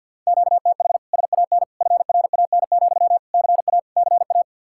Morse code, heard as a keyed tone, about 64 wpm, 700 Hz. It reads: QTH SAN FRAN, CA CA